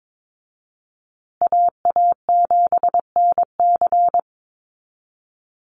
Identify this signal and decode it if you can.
Morse code, heard as a keyed tone, about 22 wpm, 700 Hz. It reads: AA7NC
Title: USA Call Signs Repeated 3x 22wpm